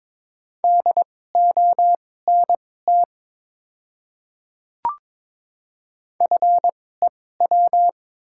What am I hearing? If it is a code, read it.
Morse code, heard as a keyed tone, about 22 wpm, spaced as for 16 wpm, 690 Hz. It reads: DONT FEW
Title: 2x Character Spacing - Top 500 Words - 22wpm